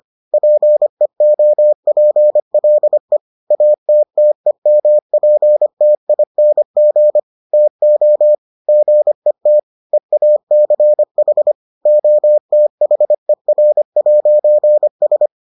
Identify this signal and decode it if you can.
Morse code, heard as a keyed tone, about 25 wpm, 600 Hz. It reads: PEOPLE ATTEMPTING TO GET EACH OTHER'S